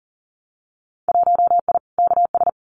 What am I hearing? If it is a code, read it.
Morse code, heard as a keyed tone, about 40 wpm, 700 Hz. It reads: 1I KS
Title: Winter Field Day Exchanges Encore 40wpm